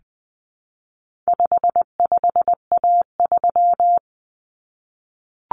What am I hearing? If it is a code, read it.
Morse code, heard as a keyed tone, about 20 wpm, 700 Hz. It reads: 55A3
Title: Common Cut Numbers - 4-Digit 20wpm